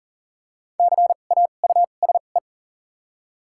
Morse code, heard as a keyed tone, about 40 wpm, spaced as for 29 wpm, 700 Hz. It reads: CAUSE
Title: 2x Character Spacing - Top 200 Words - 40wpm